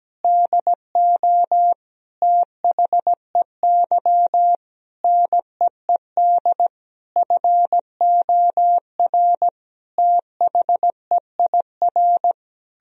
Morse code, heard as a keyed tone, about 17 wpm, 700 Hz. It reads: DO THEY NEED FOR THEIR
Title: Sentences from Top 1000 Words - Review Part 4 - 17wpm